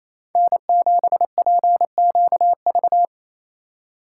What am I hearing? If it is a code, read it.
Morse code, heard as a keyed tone, about 34 wpm, 700 Hz. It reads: N7PQV
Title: Speed Racing - USA Call Signs 28wpm